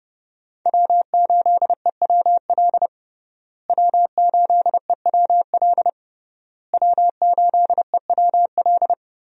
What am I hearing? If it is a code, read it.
Morse code, heard as a keyed tone, about 30 wpm, 700 Hz. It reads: W8EWL W8EWL W8EWL
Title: USA Call Signs - 5 Characters - Repeated 3x - 30wpm